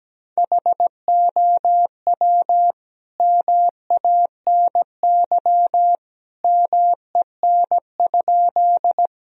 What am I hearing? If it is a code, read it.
Morse code, heard as a keyed tone, about 17 wpm, 700 Hz. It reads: HOW MANY MEN?
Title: Sentences from Top 1000 Words - Review Part 1 - 17wpm